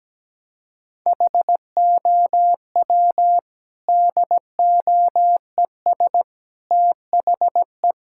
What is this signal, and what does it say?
Morse code, heard as a keyed tone, about 17 wpm, 700 Hz. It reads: HOW DOES THE
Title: Sentences from Top 1000 Words - Review Part 5 - 17wpm